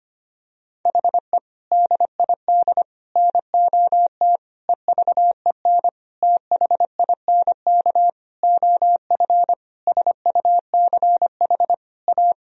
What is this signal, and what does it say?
Morse code, heard as a keyed tone, about 25 wpm, 700 Hz. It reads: HE DID NOT EVEN THINK OF SUCH A
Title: Sentences from Top 1000 Words - Review Part 1 - 25wpm